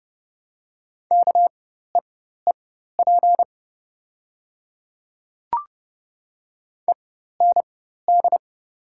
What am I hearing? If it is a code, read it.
Morse code, heard as a keyed tone, about 30 wpm, spaced as for 14 wpm, 700 Hz. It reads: KEEP END
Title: 4x Character Spacing - Top 300 Words - 30wpm